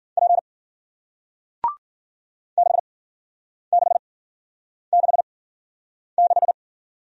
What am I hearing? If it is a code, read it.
Morse code, heard as a keyed tone, about 51 wpm, spaced as for 9 wpm, 700 Hz. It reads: P 6666